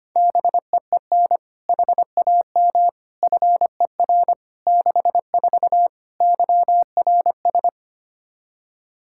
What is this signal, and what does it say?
Morse code, heard as a keyed tone, about 25 wpm, 700 Hz. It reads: BEEN HAM FER 64 YRS
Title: K1USN SST QSO Practice 25wpm